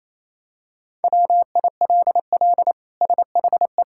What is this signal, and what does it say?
Morse code, heard as a keyed tone, about 28 wpm, 700 Hz. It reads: WILL SHE
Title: Sentences from Top 1000 Words - Review Part 6 - 28wpm